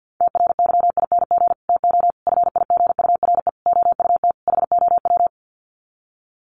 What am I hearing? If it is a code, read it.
Morse code, heard as a keyed tone, about 50 wpm, 700 Hz. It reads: TRYING TO FIGURE OUT HOW
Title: Sets of 5 Words Encore 50wpm